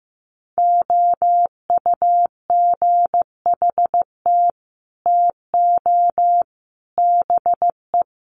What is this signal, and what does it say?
Morse code, heard as a keyed tone, about 15 wpm, 700 Hz. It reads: OUGHT TO BE
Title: Sets of 5 Words Encore2 15wpm